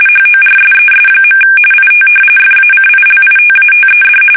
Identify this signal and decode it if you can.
RTTY, about 45.45 baud. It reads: HORSES. IT FACILITATED
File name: fsk850 / battleshipnj